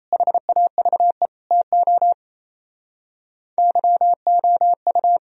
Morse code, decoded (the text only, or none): HAVE TO YOU